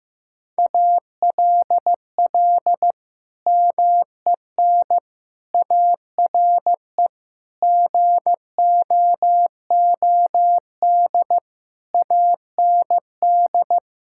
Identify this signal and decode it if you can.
Morse code, heard as a keyed tone, about 16 wpm, 700 Hz. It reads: ALL MEN ARE GOOD AND
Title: Sentences from Top 1600 Words 15wpm